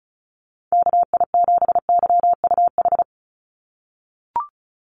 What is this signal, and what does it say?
Morse code, heard as a keyed tone, about 34 wpm, 700 Hz. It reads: KI7YUH
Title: USA Call Signs 35wpm